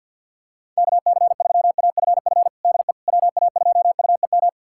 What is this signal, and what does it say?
Morse code, heard as a keyed tone, about 51 wpm, 700 Hz. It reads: KC3AFF DE WA2UEM
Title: USA Callsign DE Callsign - 6 Characters - 50wpm